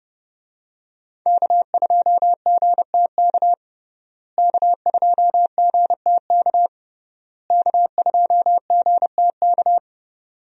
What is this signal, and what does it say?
Morse code, heard as a keyed tone, about 30 wpm, 700 Hz. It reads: K2GTK K2GTK K2GTK
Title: USA Call Signs - 5 Characters - Repeated 3x - 30wpm